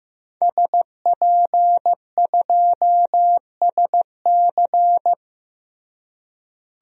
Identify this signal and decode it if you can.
Morse code, heard as a keyed tone, about 16 wpm, 700 Hz. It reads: SP2SC
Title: DXCC-100 Call Signs - 5 Characters - Repeated 3x - 15wpm